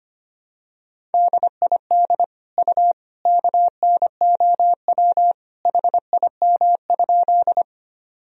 Morse code, heard as a keyed tone, about 25 wpm, 700 Hz. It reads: DID U KNOW HIM?
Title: Sentences from Top 100 Words and Abbreviations 25wpm